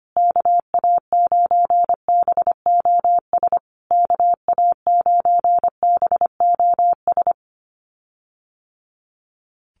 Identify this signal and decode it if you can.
Morse code, heard as a keyed tone, about 25 wpm, 700 Hz. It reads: KA9BOS KA9BOS
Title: USA Call Signs Repeated 3x 25wpm